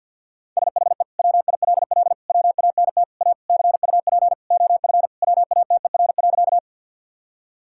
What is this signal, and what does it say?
Morse code, heard as a keyed tone, about 50 wpm, 700 Hz. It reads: SHE WILL WANT A CUP OF WATER.
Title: Sentences from Top 1600 Words 50wpm